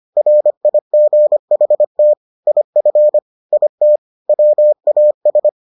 Morse code, heard as a keyed tone, about 25 wpm, 600 Hz. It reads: RIGHT IF IT WAS